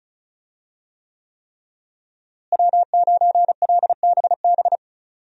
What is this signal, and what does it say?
Morse code, heard as a keyed tone, about 36 wpm, 700 Hz. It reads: W9LBB
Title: USA Call Signs - 5 Characters - Repeated 3x - 35wpm